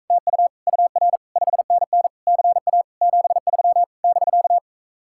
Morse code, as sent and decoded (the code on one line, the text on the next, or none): - ..- / ..- .-. / ..... -. -. / -.-. .- / --... ...-- / -...-.-
TU UR 5NN CA 73 <BK>